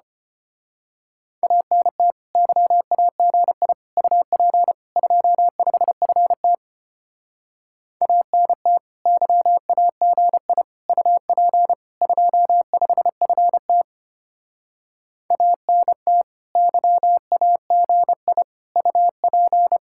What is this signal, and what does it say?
Morse code, heard as a keyed tone, about 29 wpm, 700 Hz. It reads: ANT YAGI UP 25FT ANT YAGI UP 25FT ANT YAGI UP
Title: Speed-Racing Common QSO Phrases - 25wpm